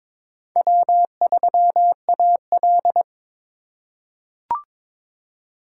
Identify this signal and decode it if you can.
Morse code, heard as a keyed tone, about 22 wpm, 700 Hz. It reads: W3AL
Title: USA Call Signs - 4 Characters - 2x Character Spacing - 22wpm